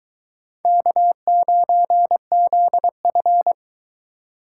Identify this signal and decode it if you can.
Morse code, heard as a keyed tone, about 20 wpm, 700 Hz. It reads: K9ZF